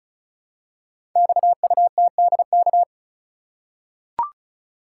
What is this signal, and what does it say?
Morse code, heard as a keyed tone, about 36 wpm, 700 Hz. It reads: XUTDK
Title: Code Groups - 5 Letters - 35wpm